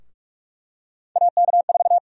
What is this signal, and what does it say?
Morse code, heard as a keyed tone, about 44 wpm, 700 Hz. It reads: AK4
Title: USA Call Sign Prefixes - Repeated 2x - 45wpm